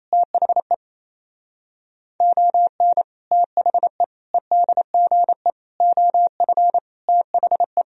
Morse code, heard as a keyed tone, about 28 wpm, 700 Hz. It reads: THE ON THE EDGE OF THE
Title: Speed-Racing - Sets of 5 Words 28wpm